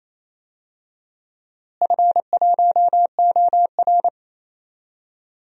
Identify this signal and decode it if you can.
Morse code, heard as a keyed tone, about 28 wpm, 700 Hz. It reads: F1OR
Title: DXCC-100 Call Signs - 4 Characters - Repeated 2x - 28wpm